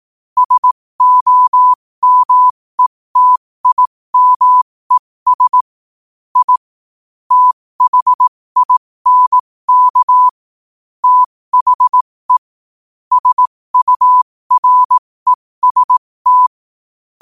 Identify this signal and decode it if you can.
Morse code, heard as a keyed tone, about 18 wpm, spaced as for 15 wpm, 1000 Hz. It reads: SOMETIMES I THINK THE SUREST
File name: qotdfiles / 15wpm